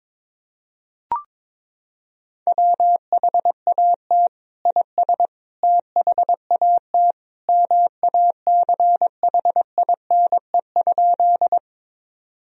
Morse code, heard as a keyed tone, about 22 wpm, 700 Hz. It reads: WHAT IS THAT MACHINE?
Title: Sentences from Top 1000 Words - Review Part 2 - 22wpm